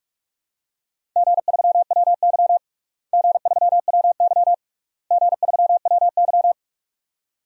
Morse code, heard as a keyed tone, about 46 wpm, 700 Hz. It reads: G3WY G3WY G3WY
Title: DXCC-100 Call Signs - 4 Characters - Repeated 3x - 45wpm